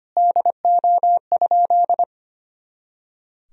Morse code, heard as a keyed tone, about 25 wpm, 700 Hz. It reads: DO?
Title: Sentences from Top 100 Words and Abbreviations 25wpm